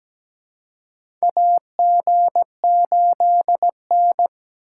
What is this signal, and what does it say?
Morse code, heard as a keyed tone, about 17 wpm, 700 Hz. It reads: AG8N
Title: USA Call Signs - 4 Characters - 2x Character Spacing - 17wpm